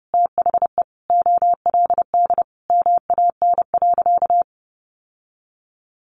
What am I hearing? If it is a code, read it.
Morse code, heard as a keyed tone, about 30 wpm, 700 Hz. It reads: THE OLD MAN.
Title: Sentences from Top 1100 Words 30wpm